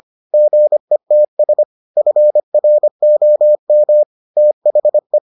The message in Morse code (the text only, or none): GETS FROM THE